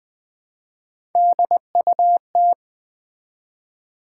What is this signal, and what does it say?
Morse code, heard as a keyed tone, about 20 wpm, 700 Hz. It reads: DUT
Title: Build A Word - 1 Letter At A Time - 4-Letter Words - 20wpm